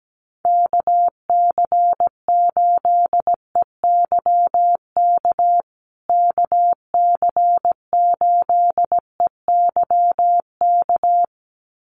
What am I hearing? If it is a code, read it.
Morse code, heard as a keyed tone, about 17 wpm, 700 Hz. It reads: KC8EYK KC8EYK
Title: USA Call Signs Repeated 2x 17wpm